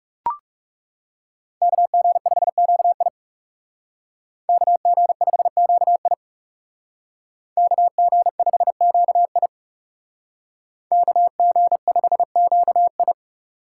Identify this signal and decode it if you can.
Morse code, heard as a keyed tone, about 37 wpm, 700 Hz. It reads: KG5QI KG5QI KG5QI KG5QI
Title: Speed Racing - USA 2x2 Call Signs 30wpm